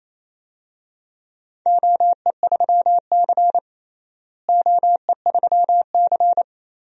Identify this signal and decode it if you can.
Morse code, heard as a keyed tone, about 28 wpm, 700 Hz. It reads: OE3C OE3C
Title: DXCC-100 Call Signs - 4 Characters - Repeated 3x - 28wpm